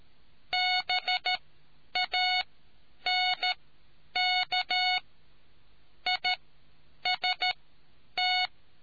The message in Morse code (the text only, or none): BANK IST